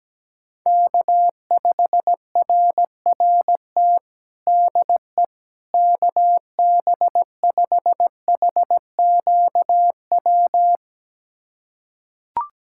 Morse code, heard as a keyed tone, about 17 wpm, 700 Hz. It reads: K5RRT DE KB5HQW
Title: USA Callsign DE Callsign 17wpm